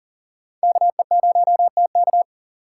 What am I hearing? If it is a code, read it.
Morse code, heard as a keyed tone, about 40 wpm, 700 Hz. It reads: KE0TK
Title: USA Call Signs - 5 Characters - 4x Character Spacing - 40wpm